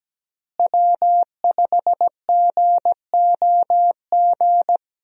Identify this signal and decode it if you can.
Morse code, heard as a keyed tone, about 17 wpm, 700 Hz. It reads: W5GOG
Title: USA Call Signs - 5 Characters - Repeated 2x - 17wpm